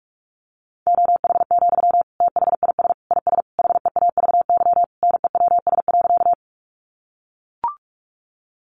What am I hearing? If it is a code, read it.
Morse code, heard as a keyed tone, about 44 wpm, 700 Hz. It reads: OH, THIS IS HEAVY NEWS.
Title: Sentences from Top 1100 Words 45wpm